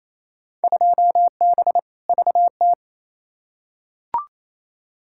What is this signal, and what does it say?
Morse code, heard as a keyed tone, about 28 wpm, 700 Hz. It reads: 2B VT
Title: Field Day Exchanges Encore - 28wpm